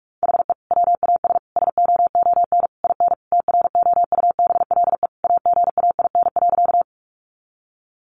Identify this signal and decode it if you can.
Morse code, heard as a keyed tone, about 44 wpm, 700 Hz. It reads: HE WAS SOON IN TROUBLE AGAIN.
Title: Sentences from Top 1200 Words 45wpm